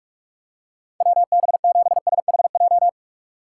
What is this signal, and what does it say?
Morse code, heard as a keyed tone, about 44 wpm, 700 Hz. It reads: WB7SHJ
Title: USA Call Signs - 6 Characters - 6x Character Spacing - 45wpm